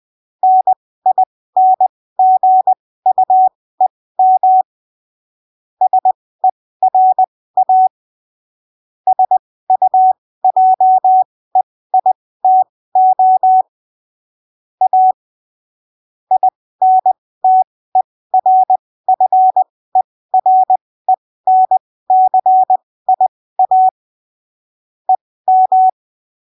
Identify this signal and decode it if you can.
Morse code, heard as a keyed tone, about 20 wpm, spaced as for 15 wpm, 750 Hz. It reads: NINGUEM SERA SUJEITO A INTERFERENCIA EM
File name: arquivos / texto / texto 28 20ppm